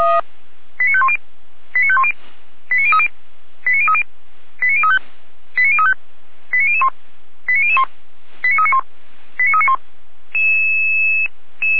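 Five-tone selective call (ZVEI), heard as quick runs of five tones, tones of about 70 ms. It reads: 89510 89510 89030 89030 89035 89035 89001 89001 89391 89391
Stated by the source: ZVEI five-tone selective call 89140 received by SDR, fire brigade feuerwehr-orlamuende.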